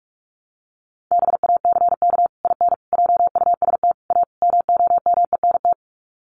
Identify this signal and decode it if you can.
Morse code, heard as a keyed tone, about 44 wpm, 700 Hz. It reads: BACK IN JUST A MOMENT
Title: Sets of 5 Words Encore 45wpm